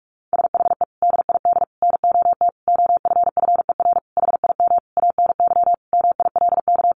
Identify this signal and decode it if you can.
Morse code, heard as a keyed tone, about 44 wpm, 700 Hz. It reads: SHE DID NOT OFFER HIM ANY MILK
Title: Sentences from Top 1500 Words 45wpm